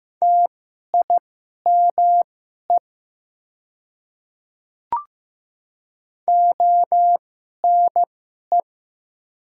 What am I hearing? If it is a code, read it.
Morse code, heard as a keyed tone, about 15 wpm, spaced as for 11 wpm, 700 Hz. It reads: TIME ONE
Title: 2x Character Spacing - Top 200 Words - 15wpm